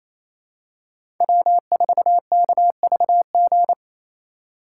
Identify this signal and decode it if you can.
Morse code, heard as a keyed tone, about 27 wpm, 700 Hz. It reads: W4KVG
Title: USA Call Signs - 4x Character Spacing - 28wpm